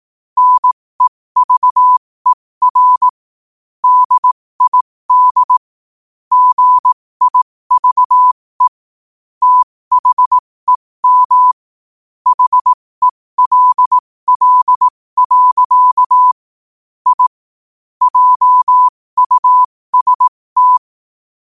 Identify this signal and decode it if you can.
Morse code, heard as a keyed tone, about 18 wpm, spaced as for 15 wpm, 1000 Hz. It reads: NEVER DID GIVE THEM HELL. I JUST
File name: qotdfiles / 15wpm